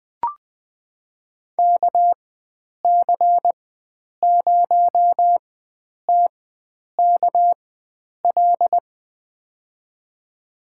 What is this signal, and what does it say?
Morse code, heard as a keyed tone, about 20 wpm, spaced as for 9 wpm, 700 Hz. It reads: KC0TKL
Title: USA Call Signs - 6 Characters - 4x Character Spacing - 20wpm